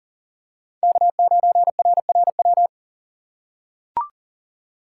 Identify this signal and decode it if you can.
Morse code, heard as a keyed tone, about 40 wpm, 700 Hz. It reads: K9RRW
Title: USA Call Signs - 5 Characters - 3x Character Spacing - 40wpm